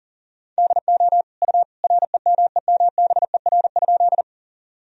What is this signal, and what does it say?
Morse code, heard as a keyed tone, about 40 wpm, 700 Hz. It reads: DO U REMEMBER?
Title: Sentences from Top 400 Words and Abbreviations 40wpm